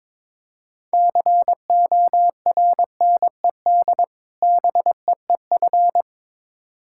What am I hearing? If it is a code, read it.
Morse code, heard as a keyed tone, about 22 wpm, 700 Hz. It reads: CORNED BEEF
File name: Difficult Words with Hints - 22wpm